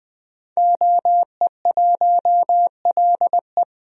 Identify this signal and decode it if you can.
Morse code, heard as a keyed tone, about 21 wpm, 700 Hz. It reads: OE1LE
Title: DXCC-100 Call Signs - 5 Characters - Repeated 3x - 20wpm